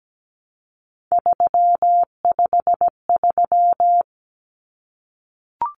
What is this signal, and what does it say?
Morse code, heard as a keyed tone, about 17 wpm, 700 Hz. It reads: 353
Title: Common Cut Numbers - 3-Digit 17wpm